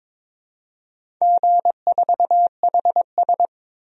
Morse code, heard as a keyed tone, about 22 wpm, 700 Hz. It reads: G4HS
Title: DXCC-100 Call Signs - 4 Characters - 22wpm